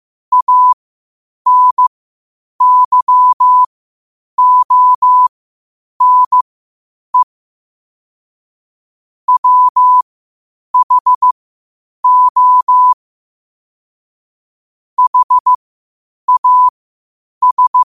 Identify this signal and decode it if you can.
Morse code, heard as a keyed tone, about 15 wpm, spaced as for 8 wpm, 1000 Hz. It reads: ANYONE WHO HAS
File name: qotdfiles / 8wpm